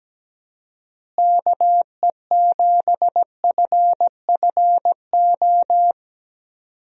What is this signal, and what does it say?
Morse code, heard as a keyed tone, about 17 wpm, 700 Hz. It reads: KE7FFO
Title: USA Call Signs - 6 Characters - Repeated 3x - 17wpm